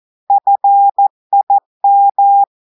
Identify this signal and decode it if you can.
Morse code, heard as a keyed tone, about 14 wpm, 800 Hz. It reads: FIM
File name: audios / portugais / portugais texte en clair 2 vitesse 14 mots minutes